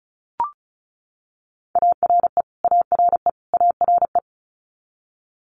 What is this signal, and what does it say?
Morse code, heard as a keyed tone, about 36 wpm, 700 Hz. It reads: ARE ARE ARE
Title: Multi-letter Phonemes Repeated 3x 35wpm